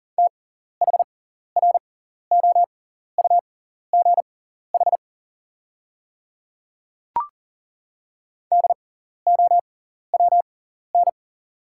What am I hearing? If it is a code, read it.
Morse code, heard as a keyed tone, about 40 wpm, spaced as for 14 wpm, 690 Hz. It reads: THROUGH DOWN
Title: 6x Character Spacing - Top 500 Words - 40wpm